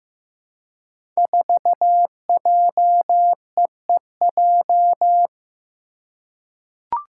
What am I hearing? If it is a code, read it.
Morse code, heard as a keyed tone, about 15 wpm, 700 Hz. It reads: 4JEEJ
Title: Five Challenging Letter-Number Combinations 15wpm